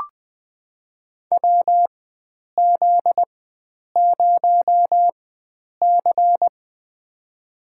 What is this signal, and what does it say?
Morse code, heard as a keyed tone, about 20 wpm, spaced as for 9 wpm, 700 Hz. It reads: WZ0C
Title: USA Call Signs - 4 Characters - 4x Character Spacing - 20wpm